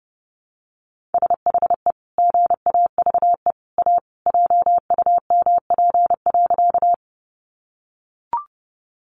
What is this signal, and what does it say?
Morse code, heard as a keyed tone, about 30 wpm, 700 Hz. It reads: SHE GAVE A JUMP.
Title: Sentences from Top 1300 Words 30wpm